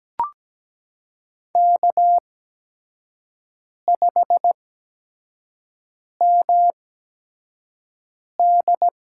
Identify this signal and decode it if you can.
Morse code, heard as a keyed tone, about 17 wpm, spaced as for 5 wpm, 700 Hz. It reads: K5MD
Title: USA Call Signs - 4 Characters - 8x Character Spacing - 17wpm